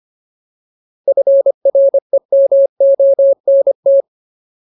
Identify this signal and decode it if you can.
Morse code, heard as a keyed tone, about 25 wpm, 550 Hz. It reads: FREMONT
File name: Ultimate Word List / 25WPM Ultimate